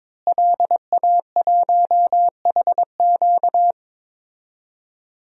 Morse code, heard as a keyed tone, about 21 wpm, 700 Hz. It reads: LA1HQ
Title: DXCC-100 Call Signs - 5 Characters - Repeated 3x - 22wpm